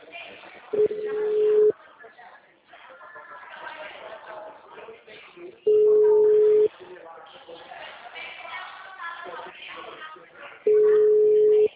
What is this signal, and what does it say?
Ringback tone (Europe, 425 Hz), with speech.
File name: out- -2004- - - .10